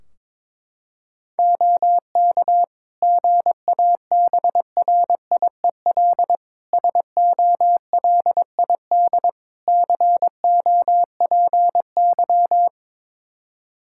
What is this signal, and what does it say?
Morse code, heard as a keyed tone, about 22 wpm, 700 Hz. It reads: OK GABRIEL SOLID COPY
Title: Practice QSO - Exchange 3 - More Info - 22wpm